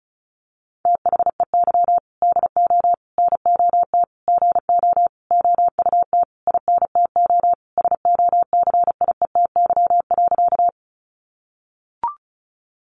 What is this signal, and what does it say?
Morse code, heard as a keyed tone, about 34 wpm, 700 Hz. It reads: THEY DO NOT GO OUT INTO SOCIETY.
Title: Sentences from Top 700 Words 35wpm